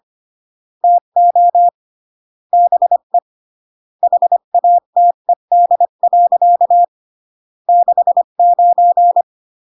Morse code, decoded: TO BE HATED. 69